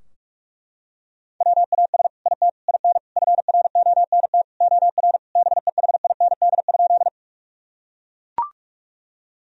Morse code, heard as a keyed tone, about 42 wpm, 700 Hz. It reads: WAS IT IN FRONT OR BEHIND?
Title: Sentences from Top 1000 Words - Review Part 2 - 45wpm